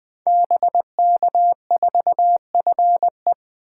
Morse code, heard as a keyed tone, about 20 wpm, 700 Hz. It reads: BK4FE
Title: Code Groups - 5 Letter:Numbers - 20wpm